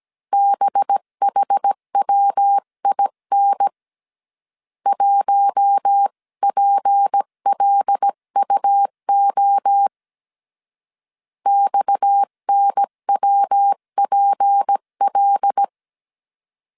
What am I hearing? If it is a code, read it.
Morse code, heard as a keyed tone, about 17 wpm, 800 Hz. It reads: BHWIN 1PLUO XNWPL
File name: CW / DET / 1 6 1500C S C 16wpm 2733